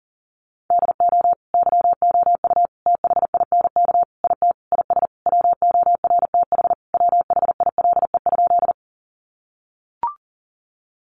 Morse code, heard as a keyed tone, about 37 wpm, 700 Hz. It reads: DO YOU THINK IT IS WORTH WHILE?
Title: Sentences from Top 800 Words 40wpm